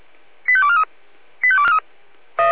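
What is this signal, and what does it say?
Five-tone selective call (ZVEI), heard as quick runs of five tones, tones of about 70 ms. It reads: 85243
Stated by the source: ZVEI five-tone call 85243 received on SDR.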